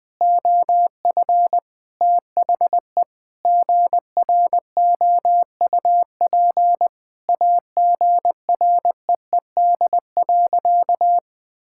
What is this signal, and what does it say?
Morse code, heard as a keyed tone, about 20 wpm, 700 Hz. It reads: OF THE GROUP AGREED.
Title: Sentences with 3-Word Phrases - Spoken Beginning - 20wpm